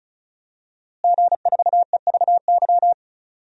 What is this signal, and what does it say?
Morse code, heard as a keyed tone, about 36 wpm, 700 Hz. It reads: G4EVY
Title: DXCC-100 Call Signs - 5 Characters - Repeated 3x - 35wpm